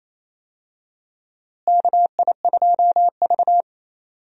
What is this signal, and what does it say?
Morse code, heard as a keyed tone, about 27 wpm, 700 Hz. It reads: KI2V
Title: USA Call Signs - 4 Characters - Repeated 3x - 28wpm